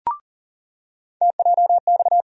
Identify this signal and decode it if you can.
Morse code, heard as a keyed tone, about 40 wpm, 700 Hz. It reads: TJX
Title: USA Call Sign Postfixes - 3 Letters 40wpm